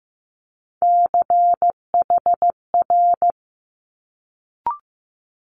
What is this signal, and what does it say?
Morse code, heard as a keyed tone, about 15 wpm, 700 Hz. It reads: CHR
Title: Multi-letter Phonemes Quick Fire 15wpm